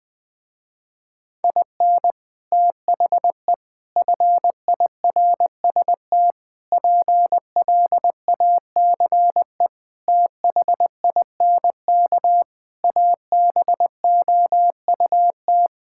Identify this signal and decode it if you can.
Morse code, heard as a keyed tone, about 20 wpm, 700 Hz. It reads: IN THE FIRST PLACE THINK ABOUT
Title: Sentences with 3-Word Phrases - Spoken Beginning - 20wpm